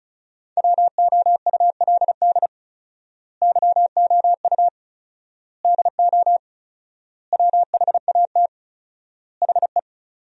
Morse code, heard as a keyed tone, about 36 wpm, 700 Hz. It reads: WOULD YOU DO WHAT HE
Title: 4x Word-Spacing - Sentences from Top 100 Words - 35wpm